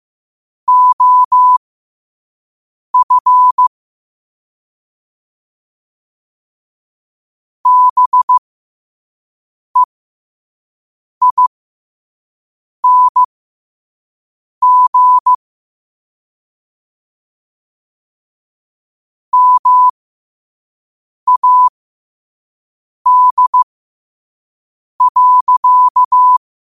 Morse code, heard as a keyed tone, about 15 wpm, spaced as for 5 wpm, 1000 Hz. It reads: OF BEING MAD.
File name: 5wpm